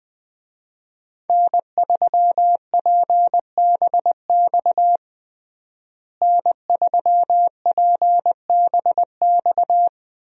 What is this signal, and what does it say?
Morse code, heard as a keyed tone, about 21 wpm, 700 Hz. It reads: N3PBX N3PBX
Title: USA Call Signs - 5 Characters - Repeated 2x - 20wpm